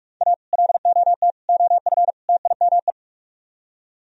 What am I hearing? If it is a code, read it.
Morse code, heard as a keyed tone, about 46 wpm, 700 Hz. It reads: A LOT OF TIME